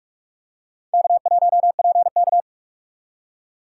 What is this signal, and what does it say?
Morse code, heard as a keyed tone, about 46 wpm, 700 Hz. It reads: K1PK
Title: USA Call Signs - 3x Character Spacing - 45wpm